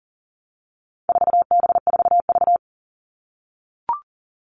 Morse code, heard as a keyed tone, about 40 wpm, 700 Hz. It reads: 4B4V
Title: All Cut Numbers - 4-Digit 40wpm